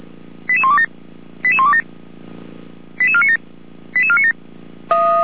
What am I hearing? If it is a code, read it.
Five-tone selective call (ZVEI), heard as quick runs of five tones, tones of about 70 ms. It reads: 88127 88127 88497 88497